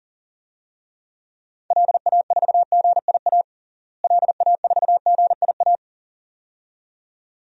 Morse code, heard as a keyed tone, about 40 wpm, 700 Hz. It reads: LA4GIA LA4GIA
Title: DXCC-100 Call Signs - 6 Characters - Repeated 2x - 40wpm